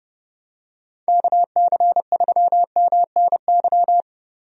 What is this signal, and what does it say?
Morse code, heard as a keyed tone, about 30 wpm, 700 Hz. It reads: KC3MNY
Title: USA Call Signs - 6 Characters - Repeated 2x - 30wpm